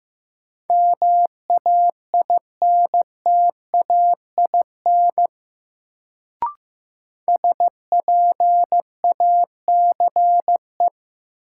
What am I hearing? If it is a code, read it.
Morse code, heard as a keyed tone, about 15 wpm, 700 Hz. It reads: MAINTAIN SPACE
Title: Sentences from Top 1000 Words - Review Part 5 - New Word Drill - 15wpm